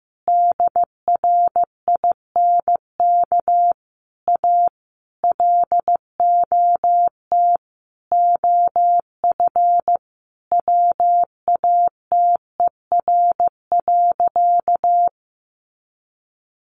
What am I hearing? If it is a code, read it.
Morse code, heard as a keyed tone, about 14 wpm, 700 Hz. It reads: DRINK A LOT OF WATER.
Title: Additional Easy Phrases 15wpm